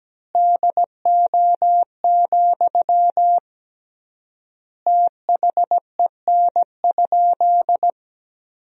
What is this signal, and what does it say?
Morse code, heard as a keyed tone, about 17 wpm, 700 Hz. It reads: DO, THEN?
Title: 3x Word-Spacing - Sentences from Top 100 Words - 17wpm